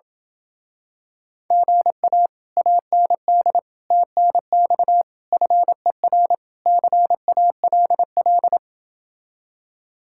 Morse code, heard as a keyed tone, about 28 wpm, 700 Hz. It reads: GA AND TNX FER CALL